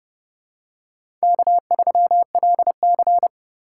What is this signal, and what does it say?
Morse code, heard as a keyed tone, about 30 wpm, 700 Hz. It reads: K3LC